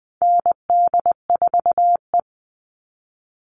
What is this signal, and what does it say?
Morse code, heard as a keyed tone, about 20 wpm, 700 Hz. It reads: ND4E
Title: All Cut Numbers - 4-Digit 20wpm